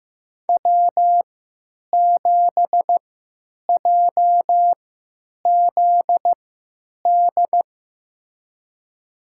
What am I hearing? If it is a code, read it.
Morse code, heard as a keyed tone, about 15 wpm, spaced as for 9 wpm, 700 Hz. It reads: W7JZD
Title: USA Call Signs - 5 Characters - 3x Character Spacing - 15wpm